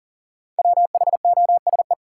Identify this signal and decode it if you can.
Morse code, heard as a keyed tone, about 37 wpm, 700 Hz. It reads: WHOSE
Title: Sentences from Top 1200 Words - New Word Drill - 40wpm